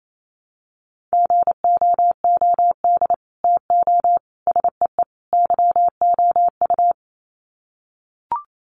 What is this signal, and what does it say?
Morse code, heard as a keyed tone, about 28 wpm, 700 Hz. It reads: GOOD TO SEE YOU
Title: Sets of 4 Words 28wpm